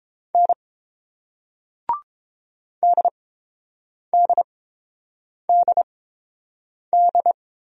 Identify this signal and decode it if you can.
Morse code, heard as a keyed tone, about 30 wpm, spaced as for 7 wpm, 700 Hz. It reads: N DDDD